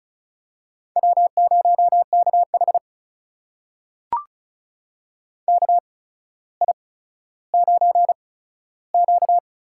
Morse code, heard as a keyed tone, about 36 wpm, spaced as for 15 wpm, 700 Hz. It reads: W0KH KI9Q
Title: USA Call Signs - 8x Character Spacing - 35wpm